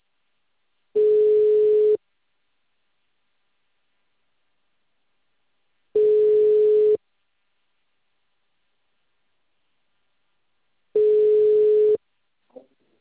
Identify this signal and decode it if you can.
Ringback tone (Europe, 425 Hz).